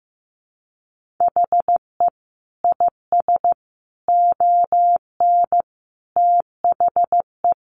Morse code, heard as a keyed tone, about 15 wpm, 700 Hz. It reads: HE IS ON THE
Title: Sentences from Top 1000 Words 15wpm